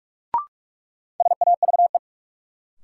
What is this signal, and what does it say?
Morse code, heard as a keyed tone, about 46 wpm, 700 Hz. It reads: SAVE
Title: Sentences from Top 1100 Words - New Word Drill - 45wpm